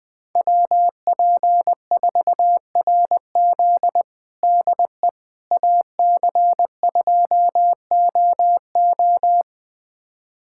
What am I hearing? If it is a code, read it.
Morse code, heard as a keyed tone, about 20 wpm, 700 Hz. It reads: WP4RZ DE AC2OO
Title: USA Callsign DE Callsign - 5 Characters - 20wpm